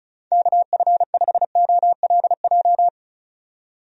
Morse code, heard as a keyed tone, about 34 wpm, 700 Hz. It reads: KF5OLJ
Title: USA Call Signs - 6 Characters - 35wpm